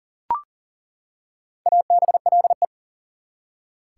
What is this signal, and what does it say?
Morse code, heard as a keyed tone, about 40 wpm, 700 Hz. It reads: ABLE
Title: Four Letter Words - 40wpm